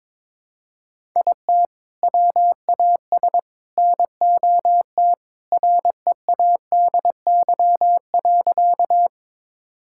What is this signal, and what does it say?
Morse code, heard as a keyed tone, about 22 wpm, 700 Hz. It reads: IT WAS NOT READY.
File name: practice / 22wpm / Sentences with 2-Word Phrases - 22wpm